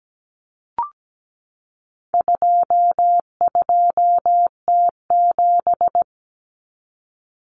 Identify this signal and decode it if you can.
Morse code, heard as a keyed tone, about 17 wpm, 700 Hz. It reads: 22T7
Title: Common Cut Numbers - 4-Digit 17wpm